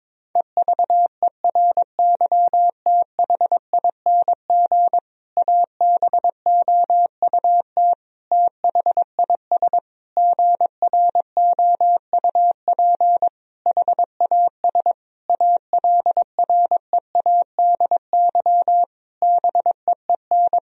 Morse code, heard as a keyed tone, about 22 wpm, 700 Hz. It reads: EVERYTHING ABOUT THIS GROUP HAS ALREADY BEEN
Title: Sentences from Top 1000 Words - Review Part 1 - 22wpm